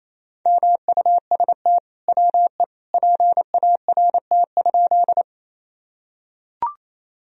Morse code, heard as a keyed tone, about 28 wpm, 700 Hz. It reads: MUST WE PART?
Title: Sentences from Top 1000 Words - Review Part 1 - 28wpm